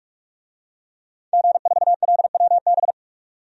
Morse code, heard as a keyed tone, about 47 wpm, 700 Hz. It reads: G4LWB
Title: DXCC-100 Call Signs - 5 Characters - 45wpm